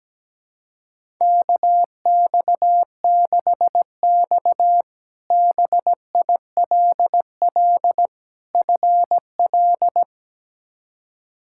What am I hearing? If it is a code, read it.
Morse code, heard as a keyed tone, about 17 wpm, 700 Hz. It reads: KX6X BILL FL
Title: K1USN SST QSO Practice - 17wpm